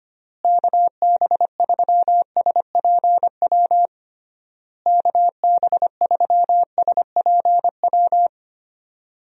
Morse code, heard as a keyed tone, about 25 wpm, 700 Hz. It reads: KB3SPW KB3SPW
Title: USA Call Signs - 6 Characters - Repeated 3x - 25wpm